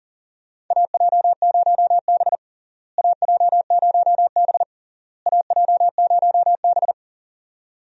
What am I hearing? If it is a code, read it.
Morse code, heard as a keyed tone, about 40 wpm, 700 Hz. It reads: AJ0B AJ0B AJ0B
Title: USA Call Signs - 4 Characters - Repeated 3x - 40wpm